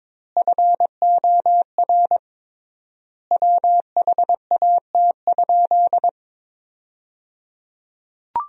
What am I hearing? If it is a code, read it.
Morse code, heard as a keyed tone, about 22 wpm, 700 Hz. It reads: FOR WHAT?
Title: 3x Word-Spacing - Sentences from Top 100 Words - 22wpm